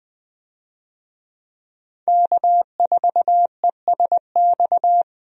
Morse code, heard as a keyed tone, about 20 wpm, 700 Hz. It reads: K4ESX
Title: USA Call Signs - 5 Characters - 20wpm